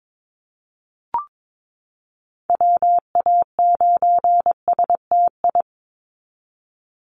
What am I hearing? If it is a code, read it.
Morse code, heard as a keyed tone, about 22 wpm, 700 Hz. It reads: WA9STI
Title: NA SOTA Activator Callsigns 22wpm